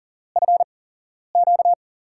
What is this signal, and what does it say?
Morse code, heard as a keyed tone, about 40 wpm, spaced as for 11 wpm, 700 Hz. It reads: FQ
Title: USA Call Signs - 4 Characters - 8x Character Spacing - 40wpm